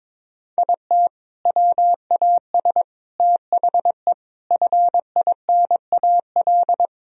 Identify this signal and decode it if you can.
Morse code, heard as a keyed tone, about 22 wpm, 700 Hz. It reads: IT WAS THE FINAL
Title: Sentences from Top 1000 Words - Review Part 2 - 22wpm